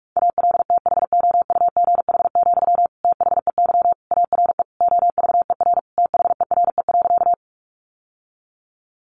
Morse code, heard as a keyed tone, about 44 wpm, 700 Hz. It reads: ALTHOUGH, THEY ARE OVER THERE.
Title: Sentences from Top 600 Words 45wpm